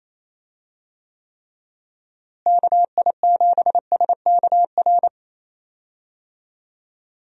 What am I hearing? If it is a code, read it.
Morse code, heard as a keyed tone, about 27 wpm, 700 Hz. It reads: KI7SKR
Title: USA Call Signs - 6 Characters - Repeated 2x - 28wpm